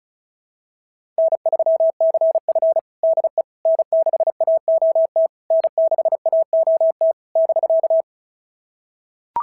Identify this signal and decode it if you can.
Morse code, heard as a keyed tone, about 31 wpm, 650 Hz. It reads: N3CF DE N6AOT N6AOT <BK>
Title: SOTA QSO Practice 35wpm